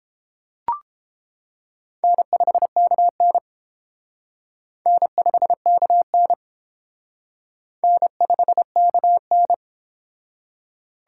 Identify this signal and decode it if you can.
Morse code, heard as a keyed tone, about 28 wpm, 700 Hz. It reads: N5KN N5KN N5KN
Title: Speed Racing - USA 1x2 Call Signs 22wpm